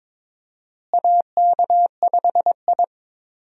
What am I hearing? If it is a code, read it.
Morse code, heard as a keyed tone, about 21 wpm, 700 Hz. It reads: AK5I
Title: USA Call Signs - 4 Characters - Repeated 3x - 22wpm